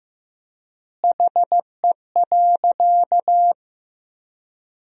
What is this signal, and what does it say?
Morse code, heard as a keyed tone, about 15 wpm, 700 Hz. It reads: HE.
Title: 3x Word-Spacing - Sentences from Top 400 Words - 15wpm